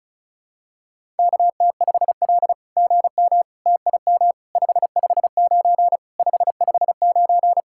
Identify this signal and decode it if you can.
Morse code, heard as a keyed tone, about 34 wpm, 700 Hz. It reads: KT5L GM TIM 559 559
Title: SOTA QSO Practice 35wpm